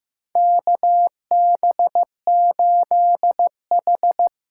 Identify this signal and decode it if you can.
Morse code, heard as a keyed tone, about 15 wpm, 700 Hz. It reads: KB8H
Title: USA Call Signs - 6 Characters - 8x Character Spacing - 15wpm